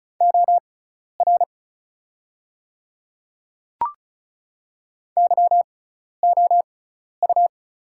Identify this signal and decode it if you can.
Morse code, heard as a keyed tone, about 36 wpm, spaced as for 12 wpm, 700 Hz. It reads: OR YOU
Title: 6x Character Spacing - Top 200 Words - 35wpm